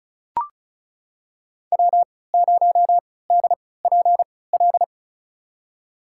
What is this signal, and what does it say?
Morse code, heard as a keyed tone, about 34 wpm, spaced as for 20 wpm, 700 Hz. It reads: W0DPL
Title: USA Call Signs - 5 Characters - 3x Character Spacing - 35wpm